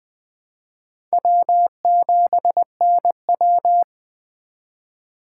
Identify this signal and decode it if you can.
Morse code, heard as a keyed tone, about 19 wpm, 700 Hz. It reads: W7NW